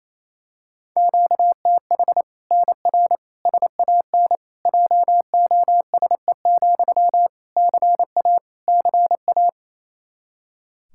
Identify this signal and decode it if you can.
Morse code, heard as a keyed tone, about 28 wpm, 700 Hz. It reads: QTH NR SAN JOSE, CA CA
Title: Practice QSO - Exchange 1 - Beginning - 28wpm